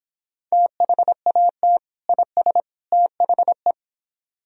Morse code, heard as a keyed tone, about 26 wpm, 700 Hz. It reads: THAT IS THE